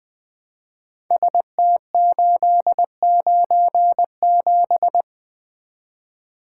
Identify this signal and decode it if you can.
Morse code, heard as a keyed tone, about 20 wpm, 700 Hz. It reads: ST897
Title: Code Groups - 5 Letter:Numbers - 20wpm